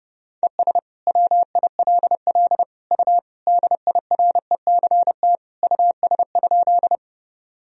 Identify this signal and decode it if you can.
Morse code, heard as a keyed tone, about 30 wpm, 700 Hz. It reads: ES WILL U DIRECT US?
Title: Sentences from Top 400 Words and Abbreviations 30wpm